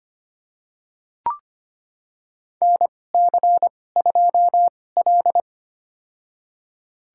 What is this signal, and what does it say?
Morse code, heard as a keyed tone, about 25 wpm, spaced as for 18 wpm, 700 Hz. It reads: NC2L
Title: USA Call Signs - 4 Characters - 2x Character Spacing - 25wpm